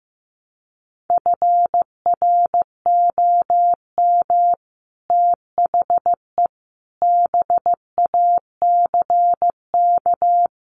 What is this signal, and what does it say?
Morse code, heard as a keyed tone, about 15 wpm, 700 Hz. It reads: FROM THE BACK
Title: Sets of 5 Words Encore 15wpm